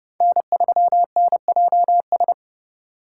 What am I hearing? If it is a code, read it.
Morse code, heard as a keyed tone, about 29 wpm, 700 Hz. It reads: N3NJS